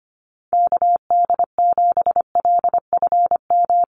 Morse code, heard as a keyed tone, about 25 wpm, 700 Hz. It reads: KD7LFM
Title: USA Call Signs Repeated 2x 25wpm